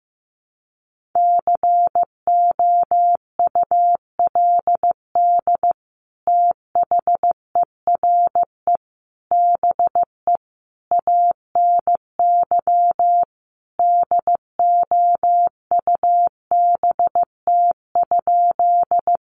Morse code, heard as a keyed tone, about 15 wpm, 700 Hz. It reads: COULD THERE BE ANY DOUBT?
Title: Sentences from Top 1500 Words 15wpm